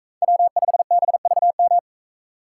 Morse code, heard as a keyed tone, about 41 wpm, 700 Hz. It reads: W5BVM